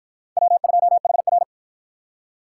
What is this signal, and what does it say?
Morse code, heard as a keyed tone, about 53 wpm, 700 Hz. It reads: W2HR